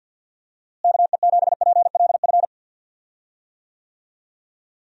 Morse code, heard as a keyed tone, about 47 wpm, 700 Hz. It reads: KE7PLF
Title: USA Call Signs - 6 Characters - Repeated 3x - 50wpm